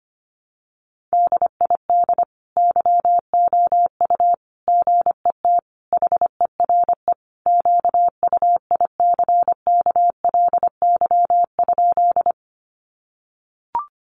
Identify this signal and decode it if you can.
Morse code, heard as a keyed tone, about 25 wpm, 700 Hz. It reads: DID YOU GET HERE QUICKLY?
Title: Sentences from Top 700 Words 25wpm